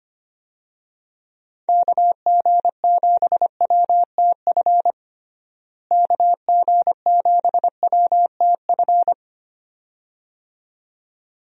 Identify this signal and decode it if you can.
Morse code, heard as a keyed tone, about 25 wpm, 700 Hz. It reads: KG7WTF KG7WTF
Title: USA Call Signs - 6 Characters - Repeated 2x - 25wpm